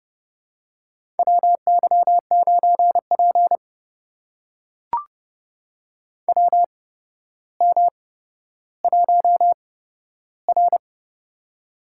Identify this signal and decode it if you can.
Morse code, heard as a keyed tone, about 30 wpm, spaced as for 13 wpm, 700 Hz. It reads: WY9P WM1R
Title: USA Call Signs - 4 Characters - 8x Character Spacing - 30wpm